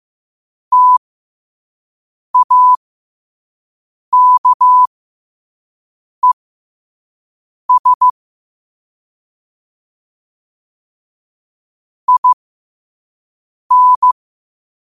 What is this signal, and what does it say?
Morse code, heard as a keyed tone, about 15 wpm, spaced as for 5 wpm, 1000 Hz. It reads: TAKES IN
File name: qotdfiles / 5wpm